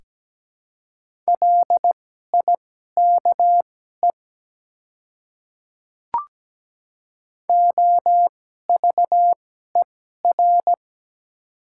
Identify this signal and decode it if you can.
Morse code, heard as a keyed tone, about 17 wpm, spaced as for 12 wpm, 700 Hz. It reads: LIKE OVER
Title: 2x Character Spacing - Top 100 Words - 17wpm